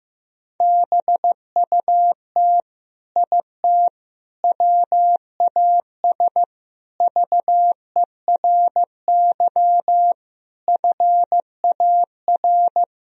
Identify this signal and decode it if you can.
Morse code, heard as a keyed tone, about 15 wpm, 700 Hz. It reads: BUT IT WAS VERY FAR
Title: Sentences from Top 1000 Words - Review Part 2 - 15wpm